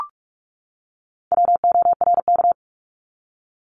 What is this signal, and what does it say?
Morse code, heard as a keyed tone, about 44 wpm, 700 Hz. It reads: WORK
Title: Top 500 Words 45wpm